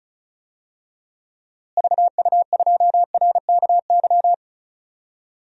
Morse code, heard as a keyed tone, about 34 wpm, 700 Hz. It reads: VU2RKY
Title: DXCC-100 Call Signs - 6 Characters - Repeated 3x - 35wpm